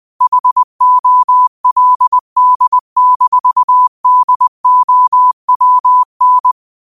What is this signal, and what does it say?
Morse code, heard as a keyed tone, about 20 wpm, 1000 Hz. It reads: HOLD-DOWN
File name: qotdfiles / 20wpm